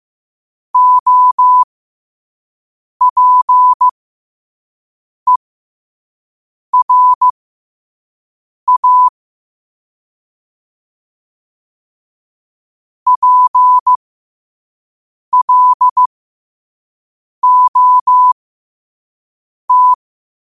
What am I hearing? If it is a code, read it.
Morse code, heard as a keyed tone, about 15 wpm, spaced as for 5 wpm, 1000 Hz. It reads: OPERA PLOT
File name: qotdfiles / 5wpm